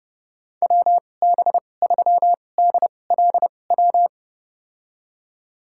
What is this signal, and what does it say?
Morse code, heard as a keyed tone, about 31 wpm, spaced as for 22 wpm, 700 Hz. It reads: WB3DLW
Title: USA Call Signs - 6 Characters - 2x Character Spacing - 30wpm